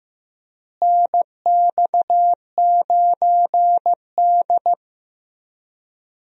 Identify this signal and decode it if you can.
Morse code, heard as a keyed tone, about 15 wpm, 700 Hz. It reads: NX9D
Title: USA Call Signs - 4 Characters - Repeated 2x - 15wpm